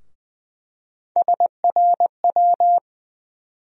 Morse code, heard as a keyed tone, about 20 wpm, 700 Hz. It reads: SRW
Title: USA Call Sign Postfixes - 3 Letters 20wpm